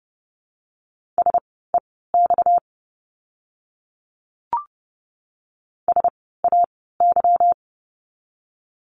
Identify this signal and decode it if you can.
Morse code, heard as a keyed tone, about 30 wpm, spaced as for 17 wpm, 700 Hz. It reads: SEX SAY
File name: course / 30:10wpm / 230 Lesson 37f - Letter x - Words 30wpm